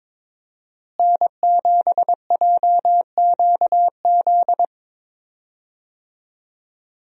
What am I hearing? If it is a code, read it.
Morse code, heard as a keyed tone, about 22 wpm, 700 Hz. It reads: N7JQZ
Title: USA Call Signs - 5 Characters - Repeated 3x - 22wpm